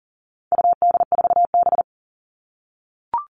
Morse code, heard as a keyed tone, about 40 wpm, 700 Hz. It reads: UD4B
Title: All Cut Numbers - 4-Digit 40wpm